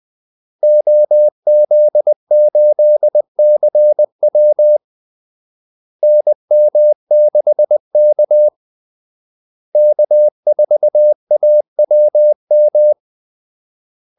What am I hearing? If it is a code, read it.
Morse code, heard as a keyed tone, about 20 wpm, 600 Hz. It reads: OZ8CW NM6K K4AWM
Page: Kaldesignaler 20 wpm